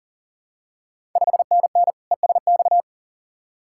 Morse code, heard as a keyed tone, about 39 wpm, 690 Hz. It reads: 5NN ESX
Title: Ontario QSO Party Exchange 40wpm